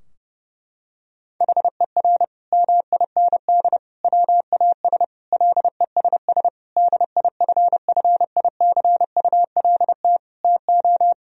Morse code, heard as a keyed tone, about 32 wpm, 700 Hz. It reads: HER MIND WAS LESS DIFFICULT TO
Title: Sentences from Top 1000 Words - Review Part 2 - 30wpm